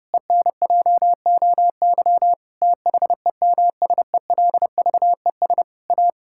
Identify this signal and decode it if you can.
Morse code, heard as a keyed tone, about 30 wpm, 700 Hz. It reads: ENJOY THEMSELVES A
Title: Sentences from Top 1000 Words - Review Part 5 - 30wpm